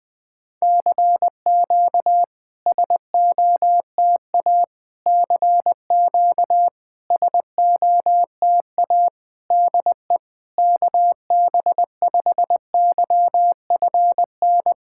Morse code, heard as a keyed tone, about 20 wpm, 690 Hz. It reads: CQ SOTA CQ SOTA DE KB5YFN
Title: SOTA QSO Practice 20wpm